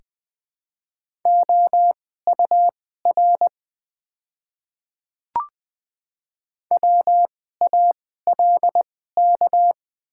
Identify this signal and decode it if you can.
Morse code, heard as a keyed tone, about 20 wpm, spaced as for 14 wpm, 700 Hz. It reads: OUR WALK
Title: 2x Character Spacing - Top 300 Words - 20wpm